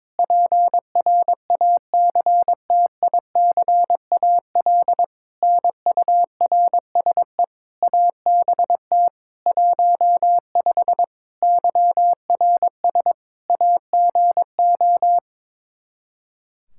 Morse code, heard as a keyed tone, about 22 wpm, 700 Hz. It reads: PRACTICAL NURSE ABT 15 YRS AGO
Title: K1USN SST QSO Practice 22wpm